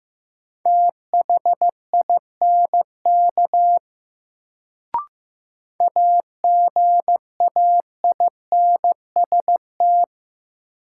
Morse code, heard as a keyed tone, about 16 wpm, 700 Hz. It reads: THINK AGAINST
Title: Sets of 2 Words - New Word Drill - 15wpm